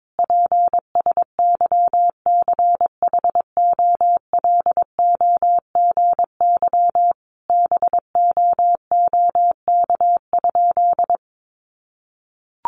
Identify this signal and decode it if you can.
Morse code, heard as a keyed tone, about 22 wpm, 700 Hz. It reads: PSYCHOLOGY BOOK?
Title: Sentences from Top 1400 Words 22wpm